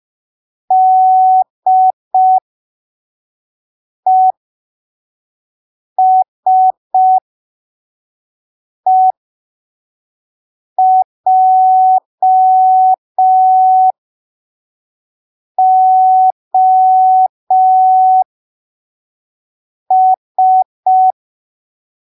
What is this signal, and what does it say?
Morse code, heard as a keyed tone, about 5 wpm, spaced as for 3 wpm, 750 Hz. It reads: DESEJOS
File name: arquivos / texto / texto 23 05ppm